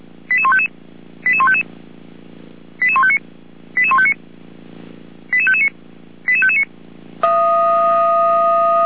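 Five-tone selective call (ZVEI), heard as quick runs of five tones, tones of about 70 ms. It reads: 88155 88155 88159 88159 88559 88559
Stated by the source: ZVEI five-tone call received by SDR.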